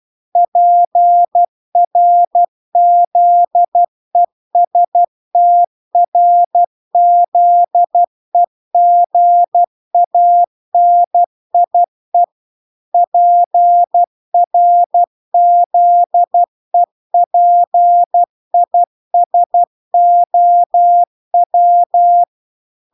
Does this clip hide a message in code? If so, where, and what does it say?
Yes: Morse code from 0.4 to 22.2 s, heard as a keyed tone, about 12 wpm, 700 Hz. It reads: PRZESTRZEGANIE PRZEPISOW